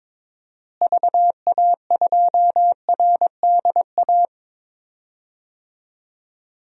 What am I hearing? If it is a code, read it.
Morse code, heard as a keyed tone, about 22 wpm, 700 Hz. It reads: VA2RDA
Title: DXCC-100 Call Signs - 6 Characters - Repeated 3x - 22wpm